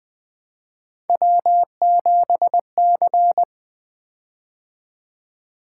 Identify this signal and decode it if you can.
Morse code, heard as a keyed tone, about 20 wpm, 700 Hz. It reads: W7C
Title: Three Letter-Number Combinations - 20wpm